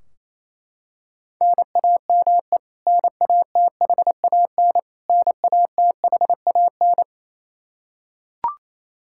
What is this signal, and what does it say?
Morse code, heard as a keyed tone, about 28 wpm, 700 Hz. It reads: NAME NATHAN NATHAN
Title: Practice QSO - Exchange 1 - Beginning - 28wpm